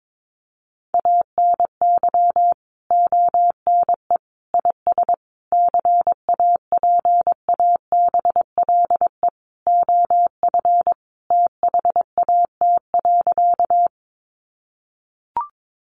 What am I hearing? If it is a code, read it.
Morse code, heard as a keyed tone, about 22 wpm, 700 Hz. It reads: ANY ONE IS CAPABLE OF THAT.
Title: Sentences from Top 1200 Words 22wpm